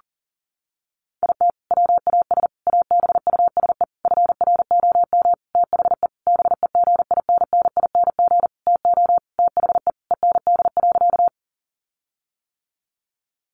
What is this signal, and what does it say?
Morse code, heard as a keyed tone, about 40 wpm, 700 Hz. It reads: IT WAS ABUSE FROM THE BEGINNING TO THE END.
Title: Sentences from Top 1300 Words 40wpm